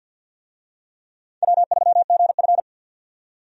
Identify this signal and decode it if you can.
Morse code, heard as a keyed tone, about 48 wpm, 700 Hz. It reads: W3GF
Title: USA Call Signs - 4 Characters - 50wpm